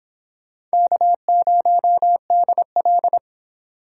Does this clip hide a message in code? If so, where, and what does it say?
Yes: Morse code from 0.7 to 3.2 s, heard as a keyed tone, about 26 wpm, 700 Hz. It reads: K0DL